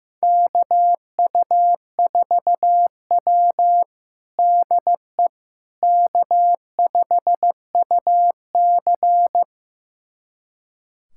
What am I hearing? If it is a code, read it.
Morse code, heard as a keyed tone, about 15 wpm, 700 Hz. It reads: KU4W DE K5UC
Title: USA Callsign DE Callsign - 4 Characters - 15wpm